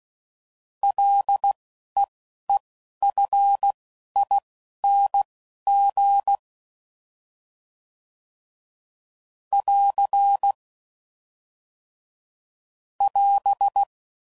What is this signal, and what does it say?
Morse code, heard as a keyed tone, about 16 wpm, spaced as for 11 wpm, 800 Hz. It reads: LEEFING <AR> <AS>
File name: Prefixes / FLs & THs / L Prefixes